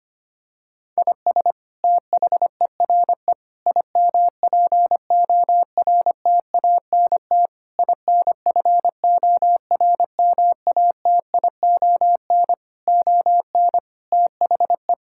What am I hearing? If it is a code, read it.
Morse code, heard as a keyed tone, about 25 wpm, 700 Hz. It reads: IS THERE IMPORTANT INFORMATION ON THE
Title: Sentences from Top 1000 Words - Review Part 6 - 25wpm